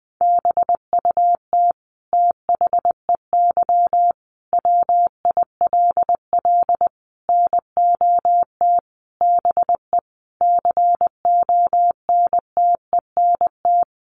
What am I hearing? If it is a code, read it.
Morse code, heard as a keyed tone, about 20 wpm, 700 Hz. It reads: BUT THEY WILL NOT BE CONTENT
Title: Sentences from Top 900 Words 20wpm